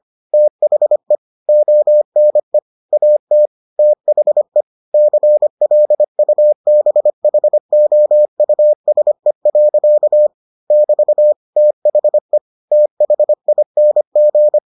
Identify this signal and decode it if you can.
Morse code, heard as a keyed tone, about 25 wpm, 600 Hz. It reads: THE ONE AT THE CLUBHOUSE. <BT> THE THING